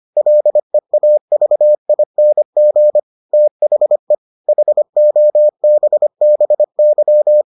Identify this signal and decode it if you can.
Morse code, heard as a keyed tone, about 25 wpm, 600 Hz. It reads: LEAVING THE HOBBY